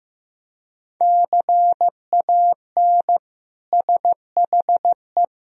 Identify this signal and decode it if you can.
Morse code, heard as a keyed tone, about 15 wpm, 700 Hz. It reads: CAN SHE
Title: Sentences from Top 1000 Words - Review Part 4 - 15wpm